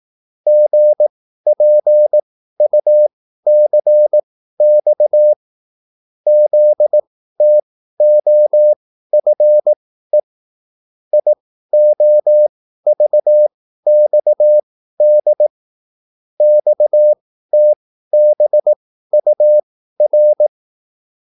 Morse code, heard as a keyed tone, about 18 wpm, spaced as for 13 wpm, 600 Hz. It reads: GPUCX ZTOFE IOVXD XTBUR
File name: QSL / CW lesson